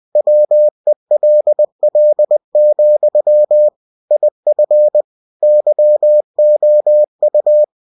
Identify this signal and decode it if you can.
Morse code, heard as a keyed tone, about 20 wpm, 600 Hz. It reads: WELL, IF YOU
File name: cw-books / winnie-the-pooh / winnie-20wpm-page-0081